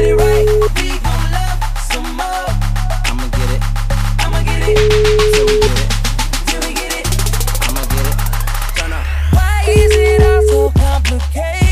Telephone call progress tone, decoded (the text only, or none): ringback tone